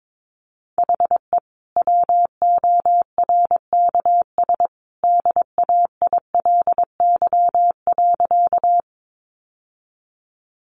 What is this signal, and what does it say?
Morse code, heard as a keyed tone, about 22 wpm, 700 Hz. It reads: HE WORKS DAILY.
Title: Sentences from Top 1100 Words 22wpm